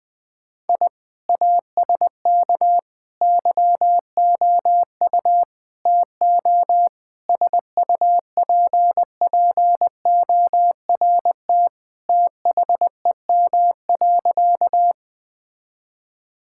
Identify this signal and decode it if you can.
Morse code, heard as a keyed tone, about 20 wpm, 700 Hz. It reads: I ASK YOU TO SUPPORT THEM.
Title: Sentences from Top 1000 Words - Review Part 5 - 20wpm